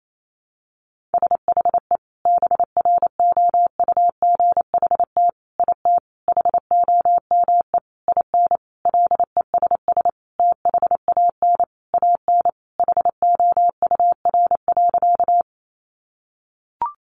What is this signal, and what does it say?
Morse code, heard as a keyed tone, about 28 wpm, 700 Hz. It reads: SHE BROUGHT IT HOME IN LESS THAN AN HOUR.
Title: Sentences from Top 500 Words 28wpm